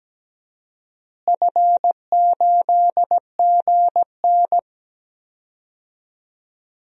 Morse code, heard as a keyed tone, about 17 wpm, 700 Hz. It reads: F8GN
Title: DXCC-100 Call Signs - 4 Characters - Repeated 3x - 17wpm